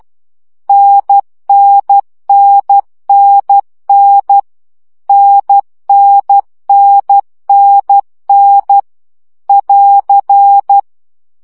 Morse code, heard as a keyed tone, about 12 wpm, 800 Hz. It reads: NNNNN NNNNN <AR>
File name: CW gyakorlas n 10 baud